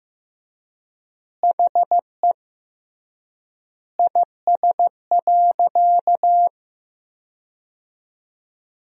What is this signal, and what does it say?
Morse code, heard as a keyed tone, about 15 wpm, 700 Hz. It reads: HE IS.
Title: 3x Word-Spacing - Sentences from Top 200 Words - 15wpm